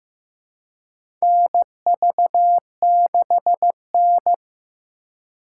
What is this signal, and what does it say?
Morse code, heard as a keyed tone, about 15 wpm, 700 Hz. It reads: NV6N
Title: USA Call Signs - 4 Characters - 4x Character Spacing - 15wpm